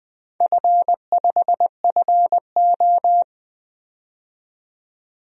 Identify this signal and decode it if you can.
Morse code, heard as a keyed tone, about 20 wpm, 700 Hz. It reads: F5FO
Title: DXCC-100 Call Signs - 4 Characters - 20wpm